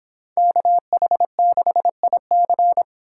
Morse code, heard as a keyed tone, about 29 wpm, 700 Hz. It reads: KH6IC